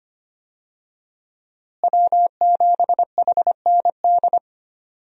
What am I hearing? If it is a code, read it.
Morse code, heard as a keyed tone, about 25 wpm, 700 Hz. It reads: W7HND
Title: USA Call Signs - 5 Characters - Repeated 2x - 25wpm